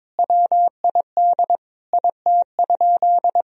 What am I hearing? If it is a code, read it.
Morse code, heard as a keyed tone, about 22 wpm, 700 Hz. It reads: WID IT?
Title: Sentences from Top 100 Words and Abbreviations 22wpm